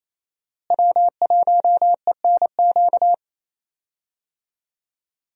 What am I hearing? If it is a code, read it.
Morse code, heard as a keyed tone, about 28 wpm, 700 Hz. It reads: W1ENQ
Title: USA Call Signs - 5 Characters - Repeated 3x - 28wpm